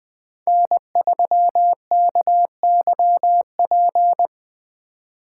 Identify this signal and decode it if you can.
Morse code, heard as a keyed tone, about 20 wpm, 700 Hz. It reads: N3KYP